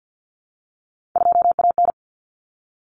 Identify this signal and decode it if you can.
Morse code, heard as a keyed tone, about 50 wpm, 700 Hz. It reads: 2AN
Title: Common Cut Numbers - 3-Digit 50wpm